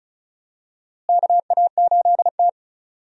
Morse code, heard as a keyed tone, about 36 wpm, 700 Hz. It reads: KA8T
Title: USA Call Signs - 4 Characters - Repeated 3x - 35wpm